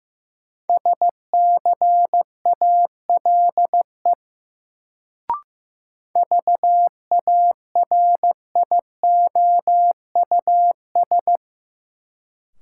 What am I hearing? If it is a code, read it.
Morse code, heard as a keyed tone, about 15 wpm, 700 Hz. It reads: SCALE VARIOUS
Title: Sentences from Top 1000 Words - Review Part 4 - New Word Drill - 15wpm